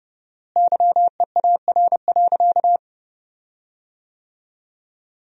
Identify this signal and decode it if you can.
Morse code, heard as a keyed tone, about 30 wpm, 700 Hz. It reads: YEAR.
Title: 4x Word-Spacing - Sentences from Top 300 Words - 30wpm